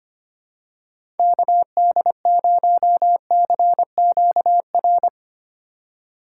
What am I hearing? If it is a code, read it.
Morse code, heard as a keyed tone, about 25 wpm, 700 Hz. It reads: KD0CQR
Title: USA Call Signs - 6 Characters - 3x Character Spacing - 25wpm